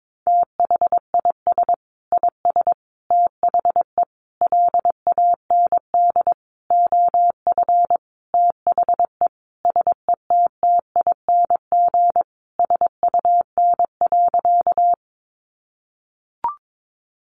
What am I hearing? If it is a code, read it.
Morse code, heard as a keyed tone, about 22 wpm, 700 Hz. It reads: THIS IS THE LAND OF THE SETTING SUN.
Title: Sentences from Top 1300 Words 22wpm